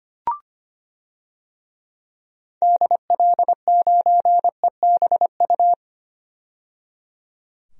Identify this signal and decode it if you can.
Morse code, heard as a keyed tone, about 25 wpm, 700 Hz. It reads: DL9EBU
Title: DXCC-100 Call Signs - 6 Characters - 25wpm